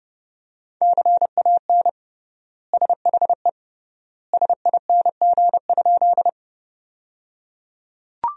Morse code, heard as a keyed tone, about 30 wpm, 700 Hz. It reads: CAN SHE SING?
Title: 3x Word-Spacing - Sentences from Top 400 Words - 30wpm